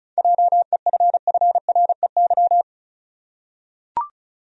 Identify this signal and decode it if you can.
Morse code, heard as a keyed tone, about 36 wpm, 700 Hz. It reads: JEFFREY
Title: USA Common Names - 35wpm